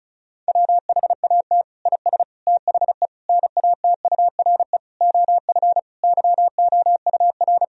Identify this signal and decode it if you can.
Morse code, heard as a keyed tone, about 36 wpm, 700 Hz. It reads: WHAT IS THE NATURE OF YOUR
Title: Sentences from Top 1000 Words - Review Part 4 - 35wpm